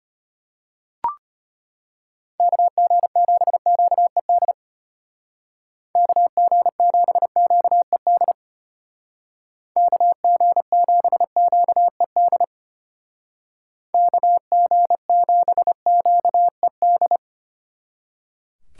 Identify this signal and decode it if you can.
Morse code, heard as a keyed tone, about 33 wpm, 700 Hz. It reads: KG7QED KG7QED KG7QED KG7QED